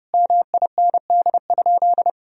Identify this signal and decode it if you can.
Morse code, heard as a keyed tone, about 30 wpm, 700 Hz. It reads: MIND?
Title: Sentences from Top 1000 Words - Review Part 2 - 30wpm